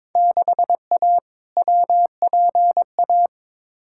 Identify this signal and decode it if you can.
Morse code, heard as a keyed tone, about 22 wpm, 700 Hz. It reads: 6A WPA
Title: Field Day Exchanges Encore - 22wpm